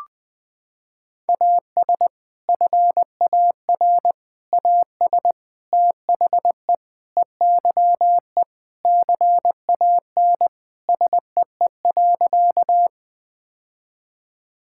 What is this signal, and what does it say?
Morse code, heard as a keyed tone, about 20 wpm, 700 Hz. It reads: AS FAR AS THE EYE CAN SEE.
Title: Sentences with 4-Word Phrases - 20wpm